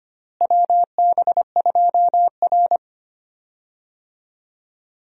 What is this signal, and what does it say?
Morse code, heard as a keyed tone, about 25 wpm, 700 Hz. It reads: WB2R
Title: USA Call Signs - 4 Characters - Repeated 3x - 25wpm